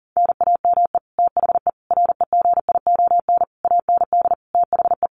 Morse code, heard as a keyed tone, about 40 wpm, 700 Hz. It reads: NAME THE REGION AND THE
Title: Sentences from Top 1500 Words 40wpm